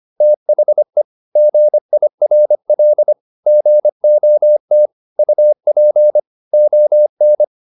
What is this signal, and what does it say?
Morse code, heard as a keyed tone, about 25 wpm, 600 Hz. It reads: THE GIRL GOT UP ON
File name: SS123 25